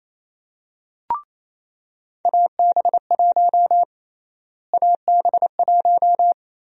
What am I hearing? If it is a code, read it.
Morse code, heard as a keyed tone, about 28 wpm, 700 Hz. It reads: AB1 AB1
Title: USA Call Sign Prefixes - Repeated 2x - 28wpm